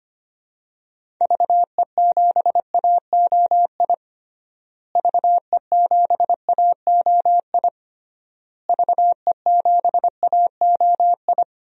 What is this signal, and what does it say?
Morse code, heard as a keyed tone, about 25 wpm, 700 Hz. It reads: VE7AOI VE7AOI VE7AOI
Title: DXCC-100 Call Signs - 6 Characters - Repeated 3x - 25wpm